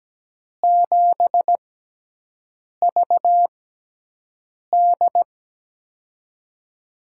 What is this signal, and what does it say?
Morse code, heard as a keyed tone, about 17 wpm, spaced as for 6 wpm, 700 Hz. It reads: 7VD
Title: USA Call Signs - 4 Characters - 6x Character Spacing - 17wpm